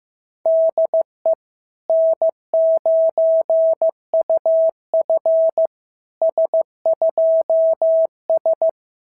Morse code, heard as a keyed tone, about 15 wpm, 650 Hz. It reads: DE N9UF S2S
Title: SOTA QSO Practice 15wpm